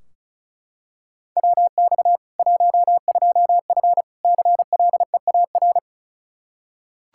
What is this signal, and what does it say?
Morse code, heard as a keyed tone, about 34 wpm, 700 Hz. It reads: WX 12F CLEAR
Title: Practice QSO - Exchange 2 - Info - 35wpm